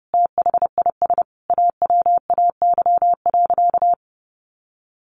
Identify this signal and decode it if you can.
Morse code, heard as a keyed tone, about 30 wpm, 700 Hz. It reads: THIS AWAY.
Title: Sentences from Top 1100 Words 30wpm